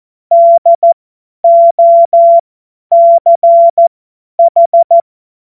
Morse code, heard as a keyed tone, about 14 wpm, spaced as for 10 wpm, 680 Hz. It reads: DOCH
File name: Witz-08-Juli-2025-50-F70